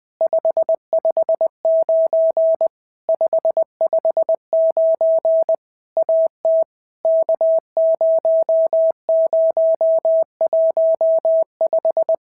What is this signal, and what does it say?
Morse code, heard as a keyed tone, about 20 wpm, 650 Hz. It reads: 559 559 AT K0015
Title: POTA QSO Practice - 20wpm